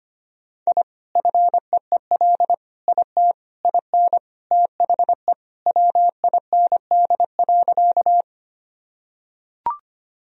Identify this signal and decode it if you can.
Morse code, heard as a keyed tone, about 25 wpm, 700 Hz. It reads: I FEEL IT IN THE WIND.
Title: Sentences from Top 1000 Words - Review Part 2 - 25wpm